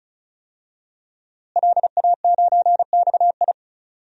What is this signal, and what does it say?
Morse code, heard as a keyed tone, about 36 wpm, 700 Hz. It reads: LA9XI
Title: DXCC-100 Call Signs - 5 Characters - Repeated 3x - 35wpm